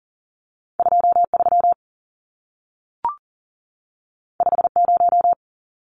Morse code, heard as a keyed tone, about 40 wpm, 700 Hz. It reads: 23 50
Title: Numbers - 2-Digit 40wpm